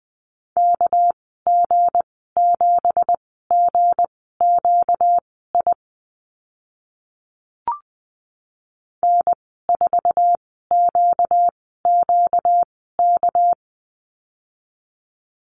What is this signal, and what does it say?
Morse code, heard as a keyed tone, about 20 wpm, spaced as for 14 wpm, 700 Hz. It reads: KG7GQI N4QQK
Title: 238 Lesson 38g - Letter q - Call Signs 20wpm